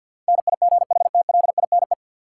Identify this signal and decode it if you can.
Morse code, heard as a keyed tone, about 54 wpm, 700 Hz. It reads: NIGHTLINE
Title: Sets of 5 Words - New Word Drill - 50wpm